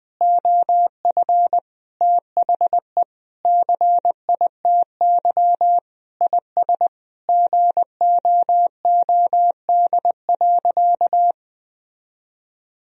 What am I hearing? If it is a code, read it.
Morse code, heard as a keyed tone, about 20 wpm, 700 Hz. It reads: OF THE CITY IS GOOD.
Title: Sentences from Top 1000 Words - Review Part 4 - 20wpm